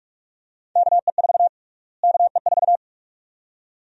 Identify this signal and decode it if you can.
Morse code, heard as a keyed tone, about 46 wpm, 700 Hz. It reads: KE4 KE4
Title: USA Call Sign Prefixes - Repeated 2x - 45wpm